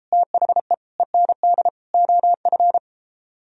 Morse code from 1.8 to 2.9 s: OF

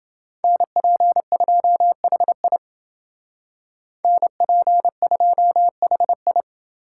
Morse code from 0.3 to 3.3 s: NP2HI

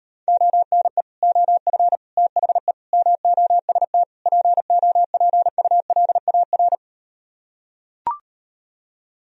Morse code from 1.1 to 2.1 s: OF

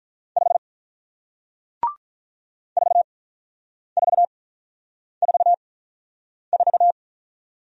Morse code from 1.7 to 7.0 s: 4444

